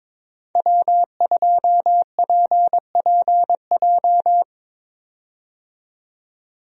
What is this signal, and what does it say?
Morse code, heard as a keyed tone, about 22 wpm, 700 Hz. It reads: W2PPJ
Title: USA Call Signs - 5 Characters - 22wpm